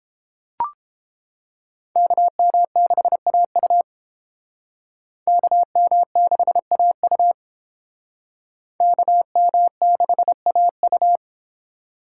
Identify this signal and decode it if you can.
Morse code, heard as a keyed tone, about 30 wpm, 700 Hz. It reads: KM6AU KM6AU KM6AU